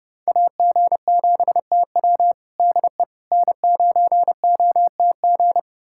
Morse code, heard as a keyed tone, about 32 wpm, 700 Hz. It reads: AG7TW DE N9OTG
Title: USA Callsign DE Callsign - 5 Characters - 30wpm